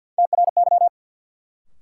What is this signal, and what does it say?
Morse code, heard as a keyed tone, about 53 wpm, 700 Hz. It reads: TRY
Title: Sentences from Top 200 Words - New Word Drill - 50wpm